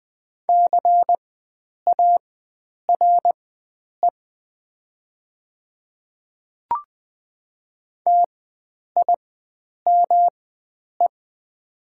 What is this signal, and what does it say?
Morse code, heard as a keyed tone, about 20 wpm, spaced as for 9 wpm, 690 Hz. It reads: CARE TIME
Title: 4x Character Spacing - Top 500 Words - 20wpm